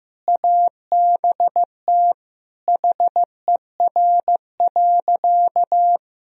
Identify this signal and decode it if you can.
Morse code, heard as a keyed tone, about 15 wpm, 700 Hz. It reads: ABT HER.
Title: Sentences from Top 200 Words and Abbreviations 15wpm